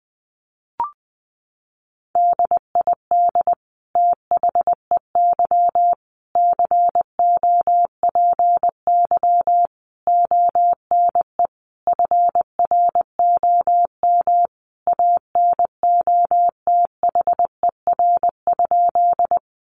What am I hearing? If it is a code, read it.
Morse code, heard as a keyed tone, about 20 wpm, 700 Hz. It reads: DID THEY COPY ONE FROM ANOTHER?
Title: Sentences from Top 1400 Words 20wpm